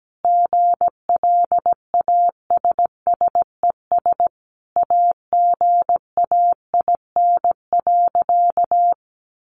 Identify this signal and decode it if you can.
Morse code, heard as a keyed tone, about 17 wpm, 700 Hz. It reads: GLASSES AGAIN.
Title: Additional Easy Phrases 17wpm